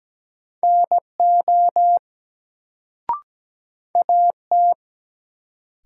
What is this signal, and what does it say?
Morse code, heard as a keyed tone, about 17 wpm, 700 Hz. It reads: NO AT
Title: Sentences from Top 100 Words - New Word Drill - 17wpm